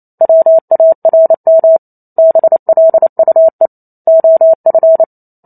Morse code, heard as a keyed tone, about 28 wpm, 650 Hz. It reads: WARM BLUE OF